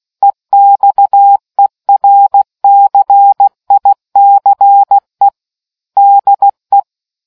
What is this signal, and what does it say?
Morse code, heard as a keyed tone, about 16 wpm, 800 Hz. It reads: EXERCICE DE